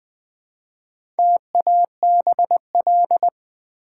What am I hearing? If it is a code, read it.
Morse code, heard as a keyed tone, about 20 wpm, 700 Hz. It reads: TABL
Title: Build A Word - 1 Letter At A Time - 3 to 5-Letter Words - 20wpm